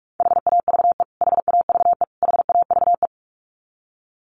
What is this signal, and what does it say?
Morse code, heard as a keyed tone, about 44 wpm, 700 Hz. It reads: HAVE HAVE HAVE
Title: Top 500 Words Repeated 45wpm